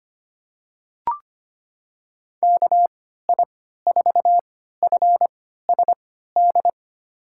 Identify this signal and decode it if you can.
Morse code, heard as a keyed tone, about 25 wpm, spaced as for 14 wpm, 700 Hz. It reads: KI4FSD
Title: USA Call Signs - 6 Characters - 3x Character Spacing - 25wpm